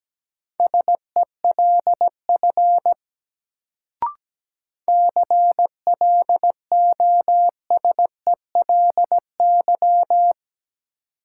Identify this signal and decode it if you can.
Morse code, heard as a keyed tone, about 17 wpm, 700 Hz. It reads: SELF CLOSELY
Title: Sentences from Top 1400 Words - New Word Drill Alt - 17wpm